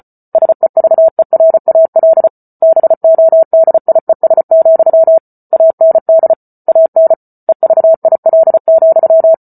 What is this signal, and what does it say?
Morse code, heard as a keyed tone, about 34 wpm, 650 Hz. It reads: SEVERAL BODIES, AND AN EVIL,